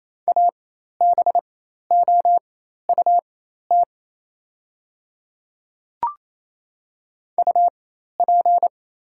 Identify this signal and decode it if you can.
Morse code, heard as a keyed tone, about 27 wpm, spaced as for 13 wpm, 690 Hz. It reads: ABOUT UP
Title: 4x Character Spacing - Top 100 Words - 28wpm